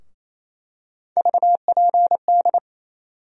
Morse code, heard as a keyed tone, about 28 wpm, 700 Hz. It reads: VPD
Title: USA Call Sign Postfixes - 3 Letters 28wpm